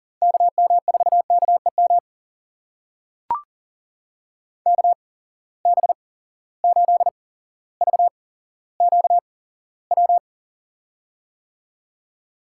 Morse code, heard as a keyed tone, about 40 wpm, spaced as for 17 wpm, 700 Hz. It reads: KM4KEM KB8VQW
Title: USA Call Signs - 6 Characters - 8x Character Spacing - 40wpm